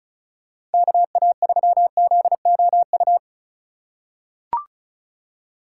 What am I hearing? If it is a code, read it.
Morse code, heard as a keyed tone, about 36 wpm, 700 Hz. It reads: KA3ZOU
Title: USA Call Signs - 6 Characters - 2x Character Spacing - 35wpm